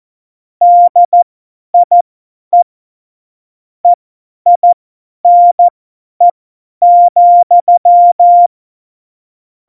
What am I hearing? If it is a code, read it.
Morse code, heard as a keyed tone, about 14 wpm, spaced as for 10 wpm, 700 Hz. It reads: DIE EINE,
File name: Morsekurs / Witze / Witz 14 April 2020, 50 F70